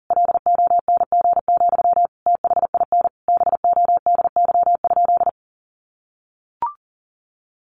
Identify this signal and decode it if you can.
Morse code, heard as a keyed tone, about 40 wpm, 700 Hz. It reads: LONG, THIN BODY?
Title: Sentences from Top 1200 Words 40wpm